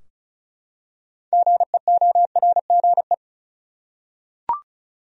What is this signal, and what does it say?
Morse code, heard as a keyed tone, about 36 wpm, 700 Hz. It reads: GEORGE
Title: USA Common Names - 35wpm